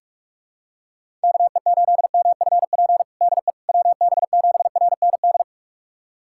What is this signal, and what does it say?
Morse code, heard as a keyed tone, about 42 wpm, 700 Hz. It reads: KE8MFP DE WB7RND
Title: USA Callsign DE Callsign - 6 Characters - 45wpm